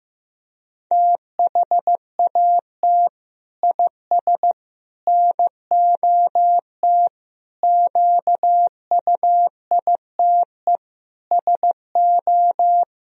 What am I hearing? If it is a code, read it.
Morse code, heard as a keyed tone, about 15 wpm, 700 Hz. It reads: THAT IS NOT QUITE SO
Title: Sentences from Top 1000 Words - Review Part 3 - 15wpm